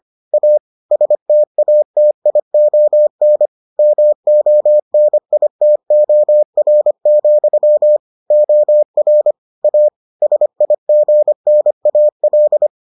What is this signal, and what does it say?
Morse code, heard as a keyed tone, about 25 wpm, 600 Hz. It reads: A STATION MONITOR, OR A SIGNAL